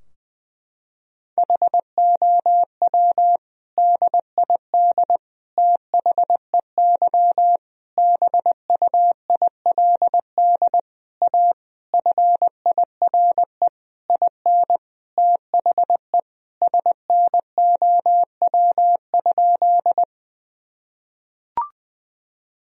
Morse code, heard as a keyed tone, about 20 wpm, 700 Hz. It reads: HOW DID THEY BUILD A FIRE IN THE SNOW?
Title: Sentences from Top 1000 Words - Review Part 3 - 20wpm